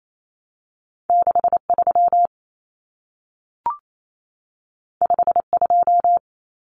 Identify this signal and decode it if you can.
Morse code, heard as a keyed tone, about 28 wpm, 700 Hz. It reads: 63 52
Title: Numbers - 2-Digit 28wpm